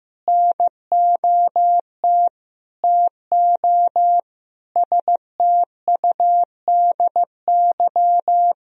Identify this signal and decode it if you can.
Morse code, heard as a keyed tone, about 15 wpm, 700 Hz. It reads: NOT TO STUDY
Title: Sentences from Top 1000 Words - Review Part 1 - 15wpm